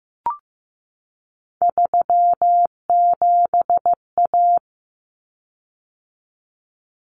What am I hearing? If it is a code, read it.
Morse code, heard as a keyed tone, about 15 wpm, 700 Hz. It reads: 37A
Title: Common Cut Numbers - 3-Digit 15wpm